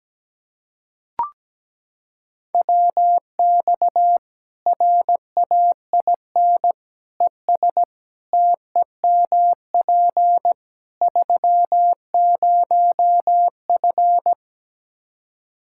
Morse code, heard as a keyed tone, about 17 wpm, 700 Hz. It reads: WX RAIN ES TEMP 30F
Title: K1USN SST QSO Practice 17wpm